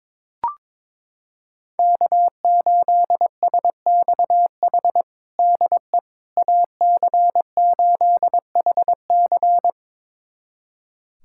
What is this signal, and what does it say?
Morse code, heard as a keyed tone, about 22 wpm, 700 Hz. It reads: K8SXH DE AC8HC
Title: USA Callsign DE Callsign 22wpm